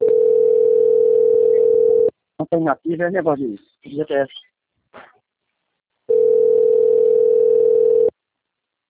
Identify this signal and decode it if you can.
Ringback tone (North America), 440 + 480 Hz, with speech.
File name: out- - - - - .13705